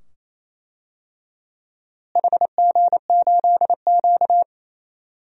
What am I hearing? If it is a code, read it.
Morse code, heard as a keyed tone, about 28 wpm, 700 Hz. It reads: HG8Q
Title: DXCC-100 Call Signs - 4 Characters - Repeated 2x - 28wpm